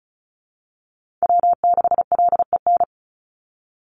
Morse code, heard as a keyed tone, about 34 wpm, 700 Hz. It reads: W6LEN
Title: Call Signs Repeated 3x - 35wpm